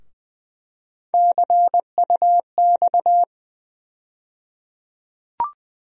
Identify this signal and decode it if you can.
Morse code, heard as a keyed tone, about 20 wpm, 700 Hz. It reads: CUX
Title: USA Call Sign Postfixes - 3 Letters 20wpm